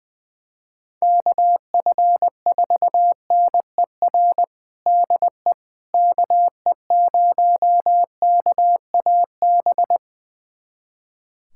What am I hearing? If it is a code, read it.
Morse code, heard as a keyed tone, about 20 wpm, 700 Hz. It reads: KF4NER DE KE0KAB
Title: USA Callsign DE Callsign 20wpm